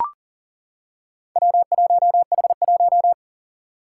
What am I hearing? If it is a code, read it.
Morse code, heard as a keyed tone, about 40 wpm, 700 Hz. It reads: W1H1
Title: Four Challenging Letter-Number Combinations 40wpm